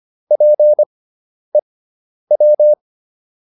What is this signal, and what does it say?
Morse code, heard as a keyed tone, about 25 wpm, spaced as for 10 wpm, 600 Hz. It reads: PEW